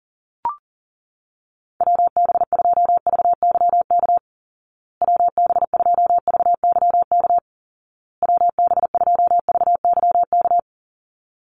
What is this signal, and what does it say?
Morse code, heard as a keyed tone, about 40 wpm, 700 Hz. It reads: WB2VYK WB2VYK WB2VYK
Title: Call Signs Repeated 2x - 40wpm